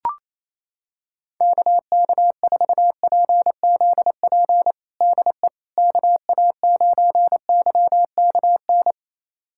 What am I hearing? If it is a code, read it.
Morse code, heard as a keyed tone, about 28 wpm, 700 Hz. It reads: KK4PZP DE KA9YKN
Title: USA Callsign DE Callsign - 6 Characters - 28wpm